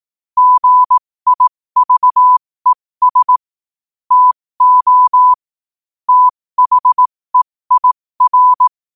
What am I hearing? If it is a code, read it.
Morse code, heard as a keyed tone, about 18 wpm, spaced as for 15 wpm, 1000 Hz. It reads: GIVES TO THEIR
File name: qotdfiles / 15wpm